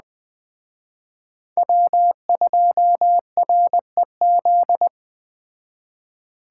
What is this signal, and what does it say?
Morse code, heard as a keyed tone, about 21 wpm, 700 Hz. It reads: W2REZ
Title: Speed Racing - USA 1x3 Call Signs 17wpm